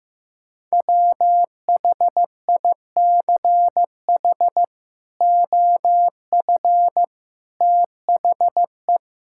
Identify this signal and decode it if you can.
Morse code, heard as a keyed tone, about 15 wpm, 700 Hz. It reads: WHICH OF THE
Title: Sentences from Top 1000 Words - Review Part 5 - 15wpm